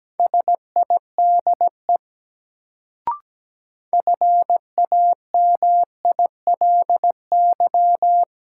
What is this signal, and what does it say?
Morse code, heard as a keyed tone, about 17 wpm, 700 Hz. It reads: SIDE FAMILY
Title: Sentences from Top 400 Words - New Word Drill - 17wpm